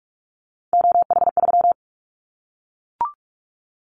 Morse code, heard as a keyed tone, about 44 wpm, 700 Hz. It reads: OH3
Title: Callsign Prefixes for Top 50 DXCC - 45wpm